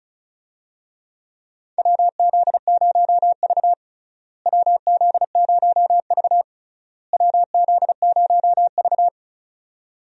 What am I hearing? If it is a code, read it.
Morse code, heard as a keyed tone, about 35 wpm, 700 Hz. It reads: WZ0V WZ0V WZ0V
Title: USA Call Signs - 4 Characters - Repeated 3x - 35wpm